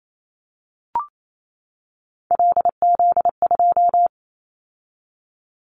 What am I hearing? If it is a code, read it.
Morse code, heard as a keyed tone, about 28 wpm, 700 Hz. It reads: LZ2
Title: Callsign Prefixes for Top 100 DXCC - 28wpm